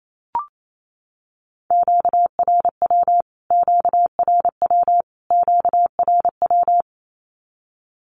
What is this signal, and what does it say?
Morse code, heard as a keyed tone, about 28 wpm, 700 Hz. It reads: QRW QRW QRW
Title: Q-Codes Repeated 3x 28wpm